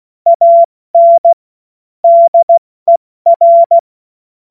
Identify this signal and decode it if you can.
Morse code, heard as a keyed tone, about 16 wpm, 680 Hz. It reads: AN DER
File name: Morsekurs / Witze / Witz 16. November 2021, 70 F80